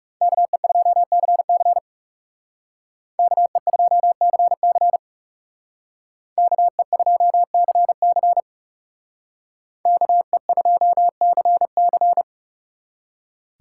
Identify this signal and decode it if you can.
Morse code, heard as a keyed tone, about 36 wpm, 700 Hz. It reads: KE2CC KE2CC KE2CC KE2CC
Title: Speed Racing - USA 2x2 Call Signs 30wpm